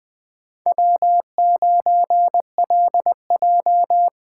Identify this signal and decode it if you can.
Morse code, heard as a keyed tone, about 24 wpm, 700 Hz. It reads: W9LJ
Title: Speed Racing - USA 1x2 Call Signs 20wpm